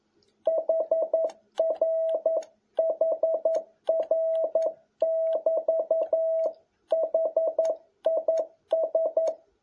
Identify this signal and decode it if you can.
Morse code, heard as a keyed tone, about 11 wpm, 650 Hz. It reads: HRHR<BT>HIS
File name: jh8cbh / 2021cw